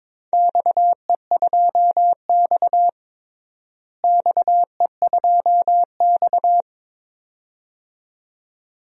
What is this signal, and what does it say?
Morse code, heard as a keyed tone, about 22 wpm, 700 Hz. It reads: XE2X XE2X
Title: DXCC-100 Call Signs - 4 Characters - Repeated 2x - 22wpm